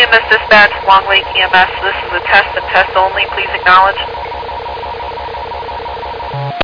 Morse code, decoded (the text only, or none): none